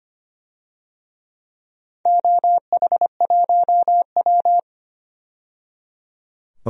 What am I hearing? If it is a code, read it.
Morse code, heard as a keyed tone, about 25 wpm, 700 Hz. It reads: OH1W
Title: DXCC-100 Call Signs - 4 Characters - 25wpm